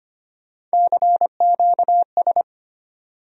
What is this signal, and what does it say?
Morse code, heard as a keyed tone, about 25 wpm, 700 Hz. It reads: CQS
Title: USA Call Sign Postfixes - 3 Letters 25wpm